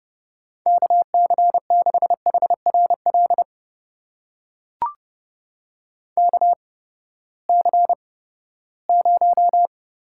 Morse code, heard as a keyed tone, about 30 wpm, spaced as for 17 wpm, 700 Hz. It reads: KC6HRL KC0
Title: USA Call Signs - 6 Characters - 8x Character Spacing - 30wpm